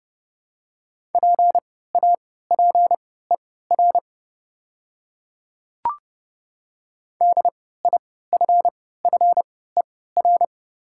Morse code, heard as a keyed tone, about 30 wpm, spaced as for 17 wpm, 700 Hz. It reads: PAPER DIFFER
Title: 3x Character Spacing - Top 300 Words - 30wpm